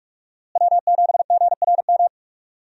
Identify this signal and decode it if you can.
Morse code, heard as a keyed tone, about 46 wpm, 700 Hz. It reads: W7GRM
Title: USA Call Signs - 5 Characters - Repeated 2x - 45wpm